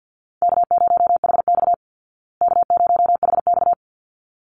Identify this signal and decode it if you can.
Morse code, heard as a keyed tone, about 48 wpm, 700 Hz. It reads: K0HX K0HX
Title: Easy Call Signs Repeated 2x - 50wpm